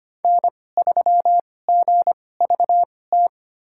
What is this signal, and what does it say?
Morse code, heard as a keyed tone, about 25 wpm, spaced as for 18 wpm, 700 Hz. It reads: N3GVT
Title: USA Call Signs - 5 Characters - 2x Character Spacing - 25wpm